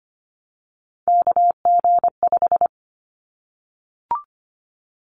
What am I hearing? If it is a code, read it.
Morse code, heard as a keyed tone, about 25 wpm, 700 Hz. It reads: KG5
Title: Callsign Prefixes for Top 25 DXCC - 25wpm